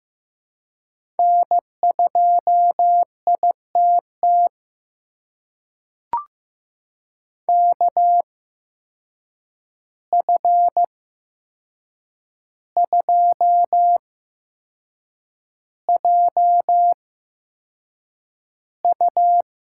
Morse code, heard as a keyed tone, about 15 wpm, spaced as for 6 wpm, 700 Hz. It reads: N2ITT KF2JU
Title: USA Call Signs - 5 Characters - 8x Character Spacing - 15wpm